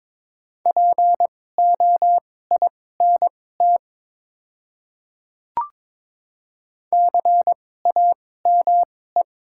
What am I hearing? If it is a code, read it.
Morse code, heard as a keyed tone, about 22 wpm, spaced as for 16 wpm, 700 Hz. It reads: POINT CAME
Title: 2x Character Spacing - Top 200 Words - 22wpm